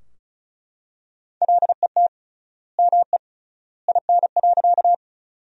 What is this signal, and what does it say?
Morse code, heard as a keyed tone, about 36 wpm, 700 Hz. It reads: LET ME IN.
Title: 3x Word-Spacing - Sentences from Top 300 Words - 35wpm